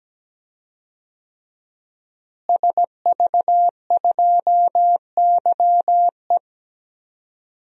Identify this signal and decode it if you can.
Morse code, heard as a keyed tone, about 17 wpm, 700 Hz. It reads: SV2YE
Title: DXCC-100 Call Signs - 5 Characters - Repeated 3x - 17wpm